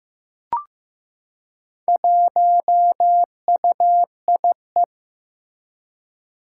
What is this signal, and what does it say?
Morse code, heard as a keyed tone, about 15 wpm, 700 Hz. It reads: 1UIE
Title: Four Challenging Letter-Number Combinations 15wpm